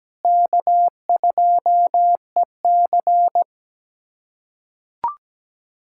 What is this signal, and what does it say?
Morse code, heard as a keyed tone, about 17 wpm, 700 Hz. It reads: K2EC
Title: USA Callsign DE Callsign - 4 Characters - 17wpm